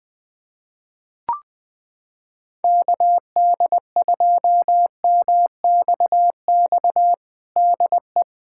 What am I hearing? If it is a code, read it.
Morse code, heard as a keyed tone, about 20 wpm, 700 Hz. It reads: KD2MXX DE
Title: USA Callsign DE Callsign 20wpm